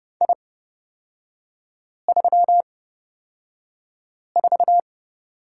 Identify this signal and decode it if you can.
Morse code, heard as a keyed tone, about 29 wpm, spaced as for 5 wpm, 700 Hz. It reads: I34
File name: Lesson 37e - Letter x - Three Characters 30wpm